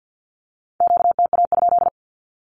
Morse code, heard as a keyed tone, about 50 wpm, 700 Hz. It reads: QTA?
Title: Q-Codes with Spoken Meaning - 50wpm